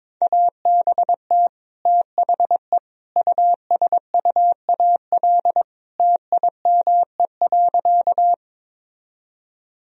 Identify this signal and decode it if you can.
Morse code, heard as a keyed tone, about 22 wpm, 700 Hz. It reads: ABT THE USUAL TIME.
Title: Sentences from Top 400 Words and Abbreviations 22wpm